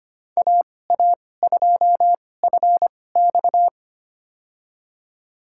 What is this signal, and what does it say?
Morse code, heard as a keyed tone, about 25 wpm, spaced as for 18 wpm, 700 Hz. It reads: AA2FX
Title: USA Call Signs - 5 Characters - 2x Character Spacing - 25wpm